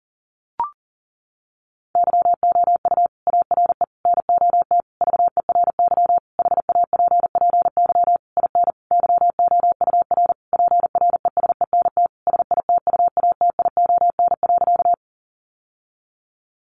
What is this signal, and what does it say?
Morse code, heard as a keyed tone, about 40 wpm, 700 Hz. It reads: YOU ARE NOT VERY HAPPY IN YOUR PRESENT SITUATION.
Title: Sentences from Top 1200 Words 40wpm